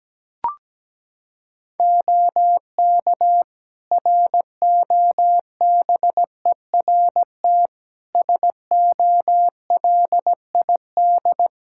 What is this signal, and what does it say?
Morse code, heard as a keyed tone, about 17 wpm, 700 Hz. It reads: OK ROBERT SOLID
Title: K1USN SST QSO Practice 17wpm